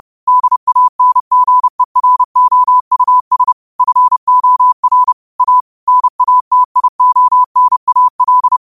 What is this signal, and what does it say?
Morse code, heard as a keyed tone, about 30 wpm, 1000 Hz. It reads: DANGEROUS FOR A NATIONAL